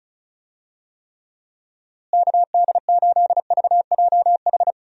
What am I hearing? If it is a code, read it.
Morse code, heard as a keyed tone, about 36 wpm, 700 Hz. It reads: KD8VJH
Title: USA Call Signs - 6 Characters - Repeated 2x - 35wpm